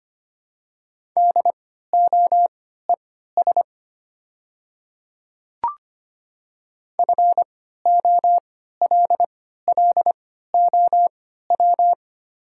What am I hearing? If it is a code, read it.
Morse code, heard as a keyed tone, about 25 wpm, spaced as for 14 wpm, 700 Hz. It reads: DOES FOLLOW
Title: 3x Character Spacing - Top 200 Words - 25wpm